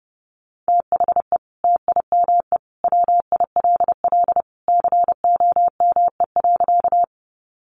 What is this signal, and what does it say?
Morse code, heard as a keyed tone, about 30 wpm, 700 Hz. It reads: THE TIME WILL COME.
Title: Sentences from Top 100 Words 30wpm